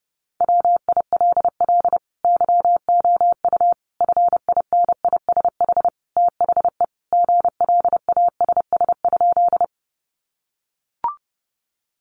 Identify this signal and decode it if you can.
Morse code, heard as a keyed tone, about 30 wpm, 700 Hz. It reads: WILL YOU FINISH THE GLASS?
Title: Sentences from Top 1300 Words 30wpm